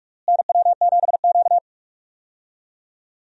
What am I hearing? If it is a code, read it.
Morse code, heard as a keyed tone, about 46 wpm, 700 Hz. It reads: NW7Q
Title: USA Call Signs - 4 Characters - Repeated 3x - 45wpm